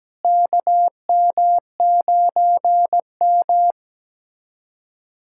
Morse code, heard as a keyed tone, about 17 wpm, 700 Hz. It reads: KM9M
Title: K1USN SST QSO Practice - 17wpm